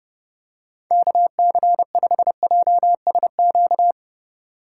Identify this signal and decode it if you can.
Morse code, heard as a keyed tone, about 32 wpm, 700 Hz. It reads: KC5JSQ
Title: USA Call Signs - 3x Character Spacing - 30wpm